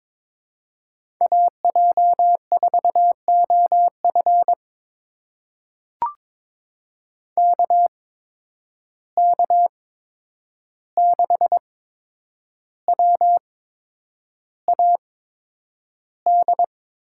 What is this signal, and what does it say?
Morse code, heard as a keyed tone, about 22 wpm, spaced as for 9 wpm, 700 Hz. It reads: AJ4OF KK6WAD
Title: USA Call Signs - 8x Character Spacing - 22wpm